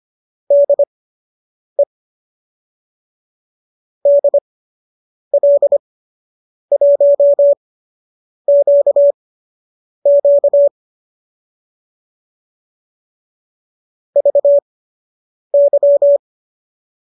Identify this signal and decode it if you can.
Morse code, heard as a keyed tone, about 25 wpm, spaced as for 8 wpm, 570 Hz. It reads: DE DL1QQ VY